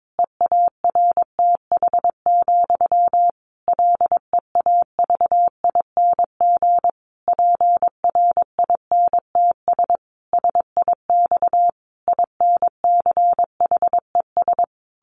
Morse code, heard as a keyed tone, about 22 wpm, 700 Hz. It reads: EARTH, LEAVING PRINTS SIX INCHES
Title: A Sound Of Thunder 22wpm